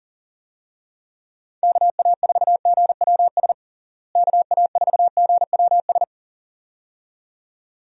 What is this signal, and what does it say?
Morse code, heard as a keyed tone, about 40 wpm, 700 Hz. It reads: KA4GWS KA4GWS
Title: USA Call Signs - 6 Characters - Repeated 2x - 40wpm